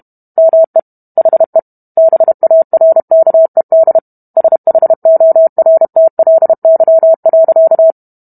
Morse code, heard as a keyed tone, about 32 wpm, 650 Hz. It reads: ME HE BARKED SHORTLY.